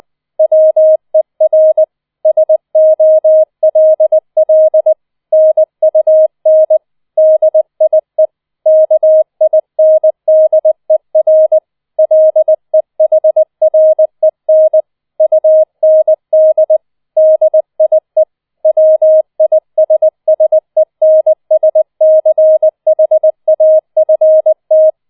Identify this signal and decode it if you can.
Morse code, heard as a keyed tone, about 20 wpm, 630 Hz. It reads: WER SOLL NUN DIE KINDER LEHREN UND DIE WISSENSCHAFT